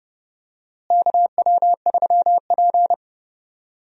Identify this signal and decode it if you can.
Morse code, heard as a keyed tone, about 31 wpm, 700 Hz. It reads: KW3P
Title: USA Call Signs - 4 Characters - Repeated 2x - 30wpm